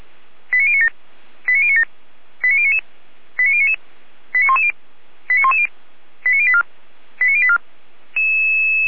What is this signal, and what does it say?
Five-tone selective call (ZVEI), heard as quick runs of five tones, tones of about 70 ms. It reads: 89097 89097 89099 89099 89110 89110 89084 89084